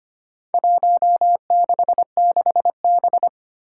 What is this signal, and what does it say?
Morse code, heard as a keyed tone, about 25 wpm, 700 Hz. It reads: 166B
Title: Four Challenging Letter-Number Combinations 25wpm